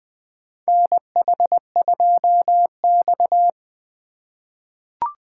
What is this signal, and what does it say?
Morse code, heard as a keyed tone, about 20 wpm, 700 Hz. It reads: NH2X
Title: USA Call Signs - 4 Characters - 4x Character Spacing - 20wpm